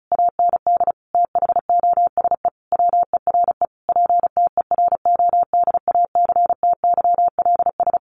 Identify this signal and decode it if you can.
Morse code, heard as a keyed tone, about 34 wpm, 700 Hz. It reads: AND THOSE WERE PTERODACTYLS
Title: A Sound Of Thunder 35wpm